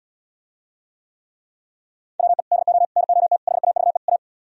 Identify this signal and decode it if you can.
Morse code, heard as a keyed tone, about 77 wpm, 700 Hz. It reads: WE DO NOT HAVE A